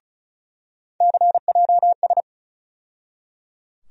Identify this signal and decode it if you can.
Morse code, heard as a keyed tone, about 34 wpm, 700 Hz. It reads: CJS
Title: USA Call Sign Postfixes - 3 Letters 35wpm